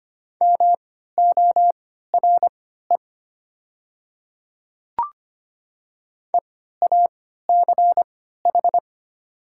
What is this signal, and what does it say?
Morse code, heard as a keyed tone, about 25 wpm, spaced as for 14 wpm, 700 Hz. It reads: MORE EACH
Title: 3x Character Spacing - Top 100 Words - 25wpm